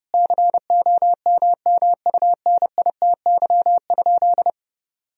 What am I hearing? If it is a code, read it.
Morse code, heard as a keyed tone, about 32 wpm, 700 Hz. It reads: COMMUNITY?
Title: Sentences from Top 1000 Words - Review Part 4 - 30wpm